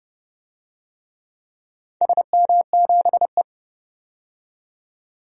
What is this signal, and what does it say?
Morse code, heard as a keyed tone, about 30 wpm, 700 Hz. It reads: SM7E
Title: DXCC-100 Call Signs - 4 Characters - 30wpm